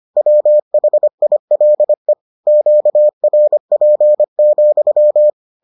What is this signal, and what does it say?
Morse code, heard as a keyed tone, about 25 wpm, 600 Hz. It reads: WHILE QRP,